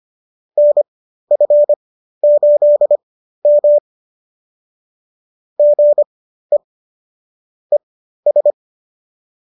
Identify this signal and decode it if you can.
Morse code, heard as a keyed tone, about 25 wpm, spaced as for 13 wpm, 590 Hz. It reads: NF8M GE ES